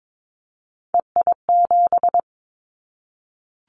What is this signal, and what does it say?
Morse code, heard as a keyed tone, about 22 wpm, 700 Hz. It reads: EI7
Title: Callsign Prefixes for Top 50 DXCC - 22wpm